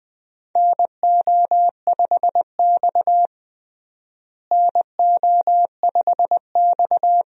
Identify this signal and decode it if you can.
Morse code, heard as a keyed tone, about 21 wpm, 700 Hz. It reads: NO5X NO5X
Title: USA Call Signs - 4 Characters - Repeated 2x - 20wpm